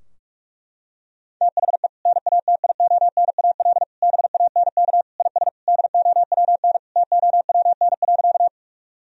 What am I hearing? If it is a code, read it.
Morse code, heard as a keyed tone, about 46 wpm, 700 Hz. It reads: THE NATIONAL BANK IS DOWN TOWN.
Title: Sentences from Top 1000 Words - Review Part 4 - 45wpm